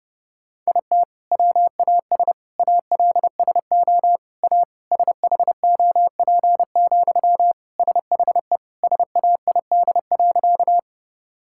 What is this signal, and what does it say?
Morse code, heard as a keyed tone, about 30 wpm, 700 Hz. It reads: IT WAS ALSO A SHOP, SHE SAID.
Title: Sentences from Top 1600 Words 30wpm